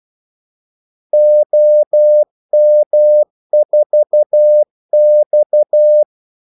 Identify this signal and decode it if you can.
Morse code, heard as a keyed tone, about 12 wpm, 600 Hz. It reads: OM4X
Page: 2 ekstra ordmellemrum mellem kaldesignalerne